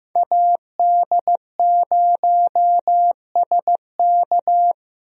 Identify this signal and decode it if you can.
Morse code, heard as a keyed tone, about 15 wpm, 700 Hz. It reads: AD0SK
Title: USA Call Signs - 5 Characters - 15wpm